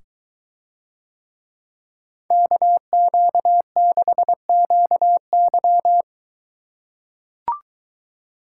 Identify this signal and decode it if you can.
Morse code, heard as a keyed tone, about 23 wpm, 700 Hz. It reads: KQ6QY